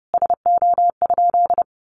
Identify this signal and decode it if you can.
Morse code, heard as a keyed tone, about 30 wpm, 700 Hz. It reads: SO?
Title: Sentences from Top 600 Words 30wpm